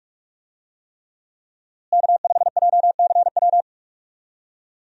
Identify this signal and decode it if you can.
Morse code, heard as a keyed tone, about 46 wpm, 700 Hz. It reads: K5JCW
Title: USA Call Signs - 5 Characters - Repeated 3x - 45wpm